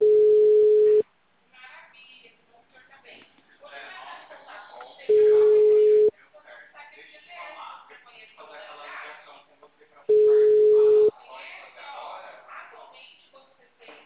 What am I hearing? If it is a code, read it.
Ringback tone (Europe, 425 Hz), with speech.